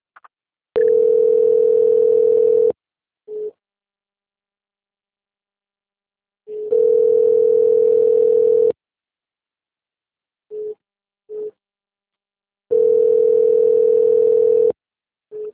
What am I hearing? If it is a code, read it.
Ringback tone (North America), 440 + 480 Hz.
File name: out- -2005- - - .17264